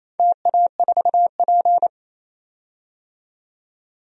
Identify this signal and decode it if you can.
Morse code, heard as a keyed tone, about 28 wpm, 700 Hz. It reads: TA4P
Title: DXCC-100 Call Signs - 4 Characters - 28wpm